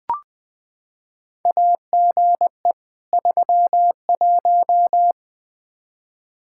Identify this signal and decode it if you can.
Morse code, heard as a keyed tone, about 20 wpm, 700 Hz. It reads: AGE 31
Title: Practice QSO - Exchange 3 - More Info - 20wpm